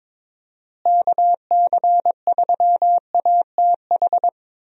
Morse code, heard as a keyed tone, about 22 wpm, 700 Hz. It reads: KC3ATH
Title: USA Call Signs - 6 Characters - Repeated 2x - 22wpm